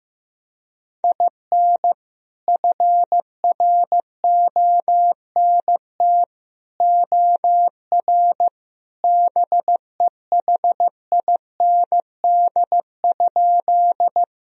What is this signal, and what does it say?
Morse code, heard as a keyed tone, about 15 wpm, 700 Hz. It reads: IN FRONT OR BEHIND?
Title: Sentences from Top 1000 Words - Review Part 2 - 15wpm